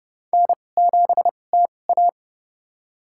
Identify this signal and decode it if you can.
Morse code, heard as a keyed tone, about 30 wpm, spaced as for 22 wpm, 700 Hz. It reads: N7TA
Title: USA Call Signs - 2x Character Spacing - 30wpm